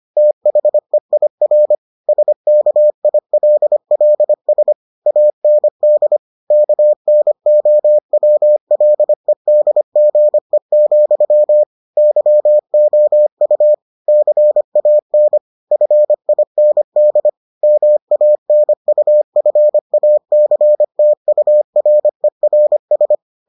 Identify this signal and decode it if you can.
Morse code, heard as a keyed tone, about 25 wpm, 600 Hz. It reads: THEIR SKILLS AND KNOWLEDGE, YOU CAN FIND MANUFACTURERS